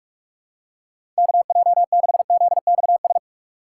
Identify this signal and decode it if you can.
Morse code, heard as a keyed tone, about 46 wpm, 700 Hz. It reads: KJ6ZXS
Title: USA Call Signs - 6x Character Spacing - 45wpm